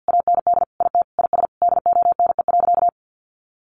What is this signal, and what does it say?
Morse code, heard as a keyed tone, about 48 wpm, 700 Hz. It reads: AND IT IS DONE.
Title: Sentences from Top 1500 Words 50wpm